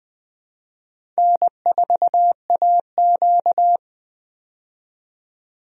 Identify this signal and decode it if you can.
Morse code, heard as a keyed tone, about 20 wpm, 700 Hz. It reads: N4AQ
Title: USA Call Signs - 4 Characters - 20wpm